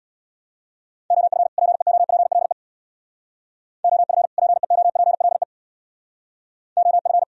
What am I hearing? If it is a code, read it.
Morse code, heard as a keyed tone, about 62 wpm, 700 Hz. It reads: OF PEOPLE OF PEOPLE OF
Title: Speed-Racing - Sets of 2 Words 50wpm